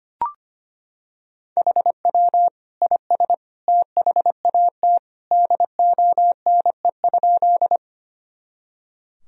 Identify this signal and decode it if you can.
Morse code, heard as a keyed tone, about 25 wpm, 700 Hz. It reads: HW IS THAT DONE?
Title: Sentences from Top 500 Words and Abbreviations 25wpm